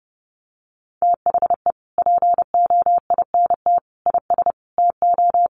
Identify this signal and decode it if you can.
Morse code, heard as a keyed tone, about 30 wpm, 700 Hz. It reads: THE POINT IS TO
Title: Sentences from Top 1300 Words 30wpm